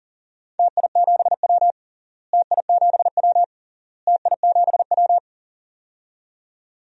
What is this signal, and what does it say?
Morse code, heard as a keyed tone, about 40 wpm, 700 Hz. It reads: TI7W TI7W TI7W
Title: DXCC-100 Call Signs - 4 Characters - Repeated 3x - 40wpm